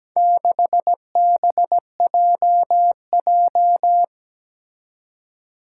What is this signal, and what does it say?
Morse code, heard as a keyed tone, about 17 wpm, 700 Hz. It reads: 6BJJ
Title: Four Challenging Letter-Number Combinations 17wpm